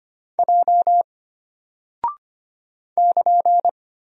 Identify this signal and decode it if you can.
Morse code, heard as a keyed tone, about 25 wpm, 700 Hz. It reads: J <KN>
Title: Single Character (Letter:Number:Punctuation:Prosign) Alt 25wpm